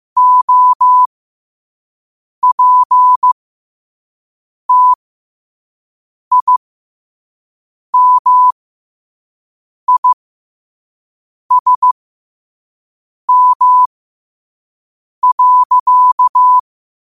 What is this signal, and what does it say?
Morse code, heard as a keyed tone, about 15 wpm, spaced as for 5 wpm, 1000 Hz. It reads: OPTIMISM.
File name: qotdfiles / 5wpm